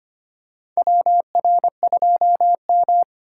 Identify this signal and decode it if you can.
Morse code, heard as a keyed tone, about 25 wpm, 700 Hz. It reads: WR2M
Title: USA Call Signs - 4 Characters - 2x Character Spacing - 25wpm